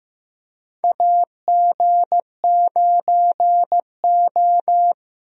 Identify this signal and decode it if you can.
Morse code, heard as a keyed tone, about 15 wpm, 700 Hz. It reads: AG9O
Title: USA Call Signs - 4 Characters - Repeated 2x - 15wpm